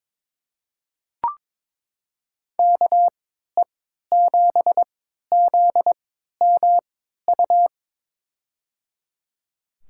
Morse code, heard as a keyed tone, about 21 wpm, spaced as for 12 wpm, 700 Hz. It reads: KE7ZMU
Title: USA Call Signs - 3x Character Spacing - 22wpm